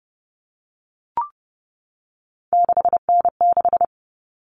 Morse code, heard as a keyed tone, about 30 wpm, 700 Hz. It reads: 6N6
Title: All Cut Numbers - 3-Digit 30wpm